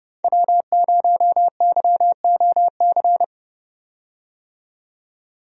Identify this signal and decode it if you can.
Morse code, heard as a keyed tone, about 31 wpm, 700 Hz. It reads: W0YOC
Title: USA Call Signs - 5 Characters - 30wpm